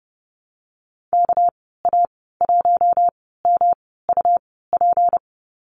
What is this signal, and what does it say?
Morse code, heard as a keyed tone, about 29 wpm, spaced as for 17 wpm, 700 Hz. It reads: KA1MUP
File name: Lesson 37g - Letter x - Call Signs 30wpm